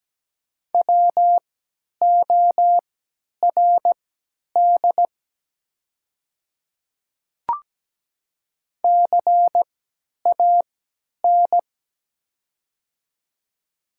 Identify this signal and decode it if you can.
Morse code, heard as a keyed tone, about 17 wpm, spaced as for 10 wpm, 700 Hz. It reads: WORD CAN
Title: 3x Character Spacing - Top 200 Words - 17wpm